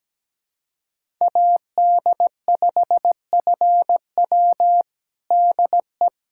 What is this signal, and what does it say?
Morse code, heard as a keyed tone, about 17 wpm, 700 Hz. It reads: AD5FW DE
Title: USA Callsign DE Callsign - 5 Characters - 17wpm